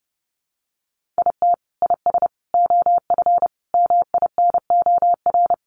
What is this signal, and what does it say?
Morse code, heard as a keyed tone, about 30 wpm, 700 Hz. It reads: IT IS OF MINOR
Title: Sentences from Top 1500 Words 30wpm